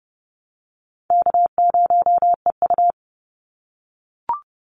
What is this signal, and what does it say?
Morse code, heard as a keyed tone, about 30 wpm, 700 Hz. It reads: K0EU
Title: Easy Call Signs - 30wpm